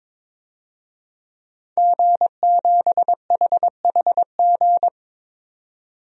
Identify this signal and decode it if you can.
Morse code, heard as a keyed tone, about 22 wpm, 700 Hz. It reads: G7HHG
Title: DXCC-100 Call Signs - 5 Characters - 22wpm